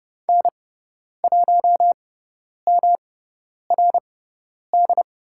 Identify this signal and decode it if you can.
Morse code, heard as a keyed tone, about 30 wpm, spaced as for 10 wpm, 700 Hz. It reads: N1MRD
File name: Lesson 18g - Letter m - Call Signs 30wpm